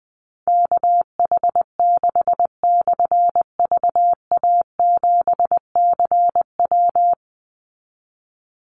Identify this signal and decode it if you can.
Morse code, heard as a keyed tone, about 20 wpm, 700 Hz. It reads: KH6/VA7CW
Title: Call Signs - 20wpm